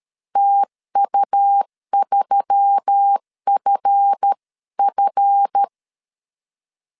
Morse code, heard as a keyed tone, about 13 wpm, 800 Hz. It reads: TU3FF
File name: LEZIONI / 7 GF9Z / 7 GF 12wpm 2122